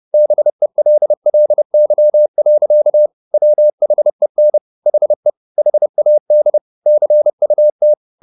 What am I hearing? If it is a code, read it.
Morse code, heard as a keyed tone, about 30 wpm, 600 Hz. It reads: BELLY. WHEN HE HAD CUT
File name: the old man and the sea / the old man and the sea 1 30wpm